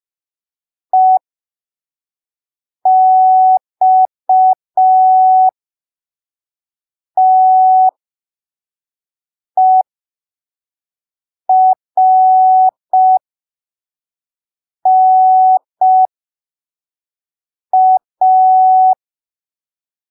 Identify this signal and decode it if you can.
Morse code, heard as a keyed tone, about 5 wpm, spaced as for 3 wpm, 750 Hz. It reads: EXTERNA
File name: arquivos / texto / texto 11 05ppm